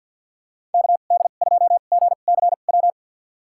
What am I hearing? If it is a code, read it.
Morse code, heard as a keyed tone, about 53 wpm, spaced as for 37 wpm, 700 Hz. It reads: KD2GCW
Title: USA Call Signs - 2x Character Spacing - 50wpm